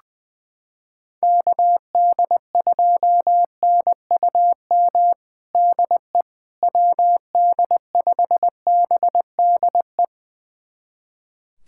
Morse code, heard as a keyed tone, about 21 wpm, 700 Hz. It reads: KD2NUM DE WD5BDE
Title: USA Callsign DE Callsign 20wpm